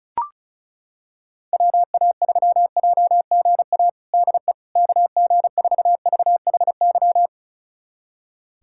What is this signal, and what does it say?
Morse code, heard as a keyed tone, about 34 wpm, 700 Hz. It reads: WA3JGA DE KG4VHY
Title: USA Callsign DE Callsign 35wpm